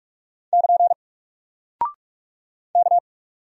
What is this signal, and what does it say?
Morse code, heard as a keyed tone, about 44 wpm, 700 Hz. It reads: <KN> K
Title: Single Character (Letter:Number:Punctuation:Prosign) 45wpm